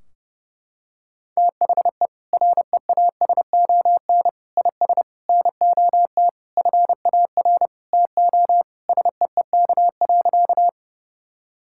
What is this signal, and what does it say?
Morse code, heard as a keyed tone, about 30 wpm, 700 Hz. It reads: THE REASON IS NOT FAR TO SEEK.
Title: Sentences from Top 1600 Words 30wpm